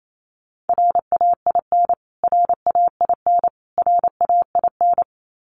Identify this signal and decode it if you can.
Morse code, heard as a keyed tone, about 28 wpm, 700 Hz. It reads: RAIN RAIN RAIN
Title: Top 500 Words Repeated 28wpm